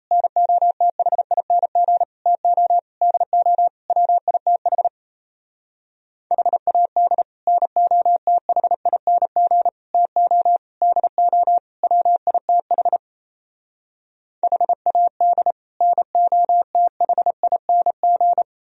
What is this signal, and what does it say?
Morse code, heard as a keyed tone, about 36 wpm, 700 Hz. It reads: NOTHING TO DO WITH HAD NOTHING TO DO WITH HAD NOTHING